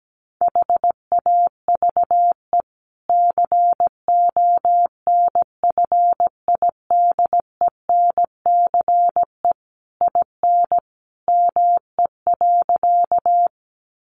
Morse code, heard as a keyed tone, about 17 wpm, 700 Hz. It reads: HAVE CONFIDENCE IN ME.
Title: Sentences from Top 1500 Words 17wpm